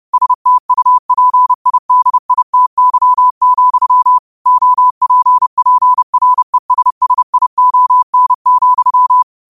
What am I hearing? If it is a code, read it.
Morse code, heard as a keyed tone, about 30 wpm, 1000 Hz. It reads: STUPIDITY, OPPRESSION,
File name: qotdfiles / 30wpm